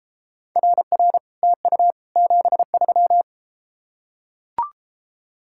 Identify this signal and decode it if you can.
Morse code, heard as a keyed tone, about 33 wpm, 700 Hz. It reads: RR TU 73